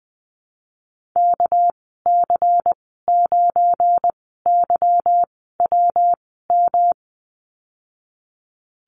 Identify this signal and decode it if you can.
Morse code, heard as a keyed tone, about 19 wpm, spaced as for 14 wpm, 700 Hz. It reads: KC9YWM
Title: 231 Lesson 37g - Letter x - Call Signs 20wpm